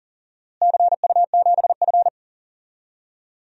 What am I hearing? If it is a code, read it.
Morse code, heard as a keyed tone, about 37 wpm, 700 Hz. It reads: CU7F
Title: DXCC-100 Call Signs - 4 Characters - Repeated 2x - 40wpm